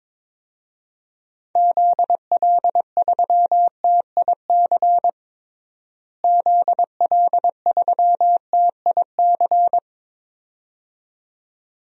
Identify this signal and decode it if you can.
Morse code, heard as a keyed tone, about 22 wpm, 700 Hz. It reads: ZL3TIC ZL3TIC
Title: DXCC-100 Call Signs - 6 Characters - Repeated 2x - 22wpm